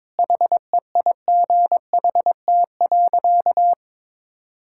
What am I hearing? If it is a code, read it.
Morse code, heard as a keyed tone, about 22 wpm, 700 Hz. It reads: HEIGHT.
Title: Sentences from Top 1600 Words 22wpm